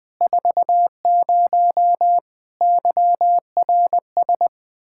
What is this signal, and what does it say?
Morse code, heard as a keyed tone, about 21 wpm, 700 Hz. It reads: 40 YRS
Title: Practice QSO - Exchange 3 - More Info - 20wpm